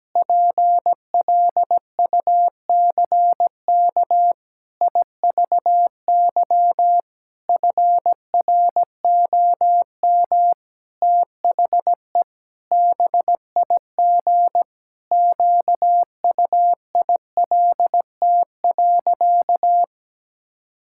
Morse code, heard as a keyed tone, about 17 wpm, 700 Hz. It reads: PLUCK IVY FROM THE BIG QUILT.
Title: Pangram Challenge v2 - 17wpm